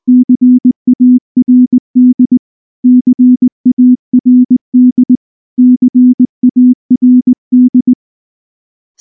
Morse code, heard as a keyed tone, about 19 wpm, 250 Hz. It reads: CARD CARD CARD
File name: morse / morse-training-20WPM